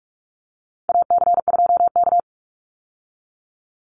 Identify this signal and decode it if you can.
Morse code, heard as a keyed tone, about 44 wpm, 700 Hz. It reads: AC2K
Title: Call Signs Repeated 3x - 45wpm